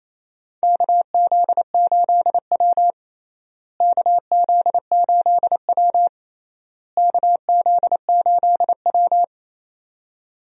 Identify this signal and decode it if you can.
Morse code, heard as a keyed tone, about 28 wpm, 700 Hz. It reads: KZ8W KZ8W KZ8W
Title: USA Call Signs - 4 Characters - Repeated 3x - 28wpm